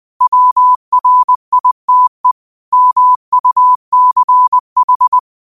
Morse code, heard as a keyed tone, about 20 wpm, 1000 Hz. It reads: WRITE MUCH
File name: qotdfiles / 20wpm